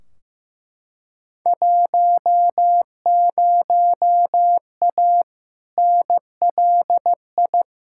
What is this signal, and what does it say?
Morse code, heard as a keyed tone, about 16 wpm, 700 Hz. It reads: 10A NLI
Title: Field Day Exchanges Encore - 15wpm